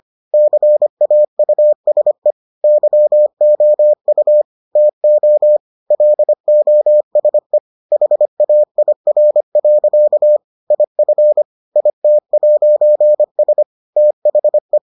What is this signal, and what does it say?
Morse code, heard as a keyed tone, about 25 wpm, 600 Hz. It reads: CAUSE YOU TO LOSE HAIR. IF IT'S THE